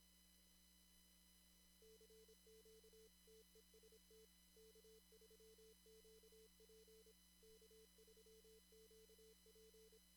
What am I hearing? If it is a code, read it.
Morse code, heard as a keyed tone, about 26 wpm, 440 Hz. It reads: CQ TEST K3QP K3QP
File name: CW / kt5j wpx cw-op-k5pi